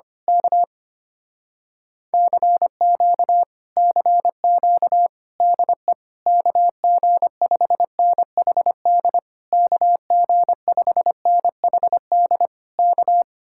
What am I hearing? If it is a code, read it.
Morse code, heard as a keyed tone, about 24 wpm, 700 Hz. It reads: K CQ CQ DE KG5NHD KG5NHD K
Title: Speed-Racing Common QSO Phrases Encore - 25wpm